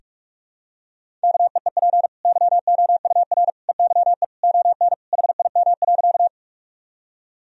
Morse code, heard as a keyed tone, about 41 wpm, 700 Hz. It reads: KEEP YOUR EYE ON HIM.
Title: Sentences from Top 1000 Words - Review Part 1 - 45wpm